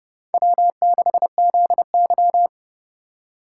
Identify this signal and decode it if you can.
Morse code, heard as a keyed tone, about 38 wpm, 700 Hz. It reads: W6ZY